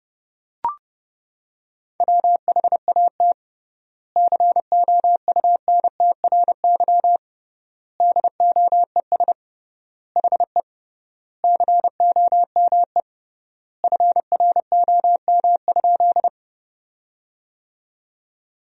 Morse code, heard as a keyed tone, about 30 wpm, 700 Hz. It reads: WHAT COUNTRY DOES HE COME FROM?
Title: 3x Word-Spacing - Sentences from Top 300 Words - 30wpm